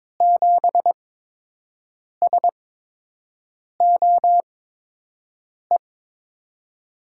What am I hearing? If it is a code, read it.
Morse code, heard as a keyed tone, about 22 wpm, spaced as for 6 wpm, 700 Hz. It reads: 7SOE
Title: USA Call Signs - 6 Characters - 8x Character Spacing - 22wpm